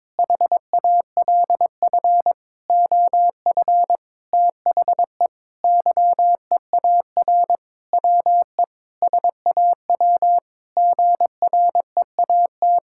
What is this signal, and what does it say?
Morse code, heard as a keyed tone, about 21 wpm, 700 Hz. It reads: HALF OF THE YEAR WE SAW GREAT
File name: Sentences with 7-Word Phrases - 22wpm